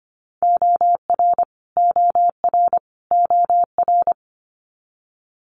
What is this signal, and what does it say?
Morse code, heard as a keyed tone, about 25 wpm, 700 Hz. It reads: OR OR OR
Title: Multi-letter Phonemes Repeated 3x 25wpm